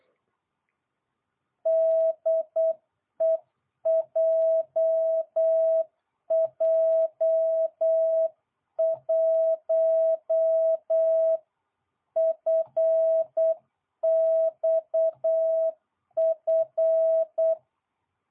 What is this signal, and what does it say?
Morse code, heard as a keyed tone, about 8 wpm, 660 Hz. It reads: DE JJ1FXF